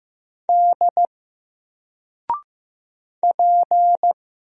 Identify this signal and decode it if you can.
Morse code, heard as a keyed tone, about 15 wpm, 700 Hz. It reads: D P
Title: Single Letter-Number Alt 15wpm